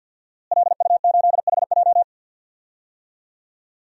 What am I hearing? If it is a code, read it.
Morse code, heard as a keyed tone, about 51 wpm, 700 Hz. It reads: LU8HJ
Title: DXCC-100 Call Signs - 5 Characters - Repeated 3x - 50wpm